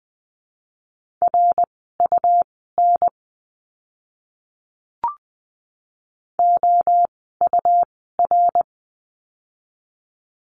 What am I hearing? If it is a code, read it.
Morse code, heard as a keyed tone, about 20 wpm, spaced as for 15 wpm, 700 Hz. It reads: RUN OUR
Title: 101 Lesson 17f - Letter u - Words 20wpm